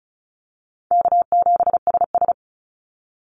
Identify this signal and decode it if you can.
Morse code, heard as a keyed tone, about 36 wpm, 700 Hz. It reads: K7SS
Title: Call Signs - 35wpm